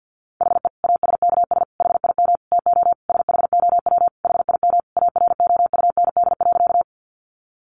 Morse code, heard as a keyed tone, about 48 wpm, 700 Hz. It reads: HE ASKS HIM TO SHOW HIM AROUND.
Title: Easy Phrases 50wpm